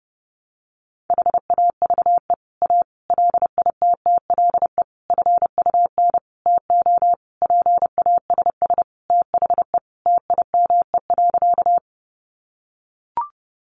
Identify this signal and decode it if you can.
Morse code, heard as a keyed tone, about 30 wpm, 700 Hz. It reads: HAVE A LITTLE FUN TO PASS THE TIME.
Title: Sentences from Top 700 Words 30wpm